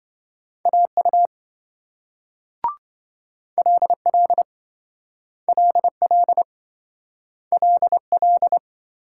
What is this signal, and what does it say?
Morse code, heard as a keyed tone, about 27 wpm, 700 Hz. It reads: AU LL LL LL